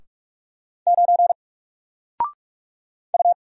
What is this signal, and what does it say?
Morse code, heard as a keyed tone, about 44 wpm, 700 Hz. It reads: 9 U
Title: Single Letter-Number 45wpm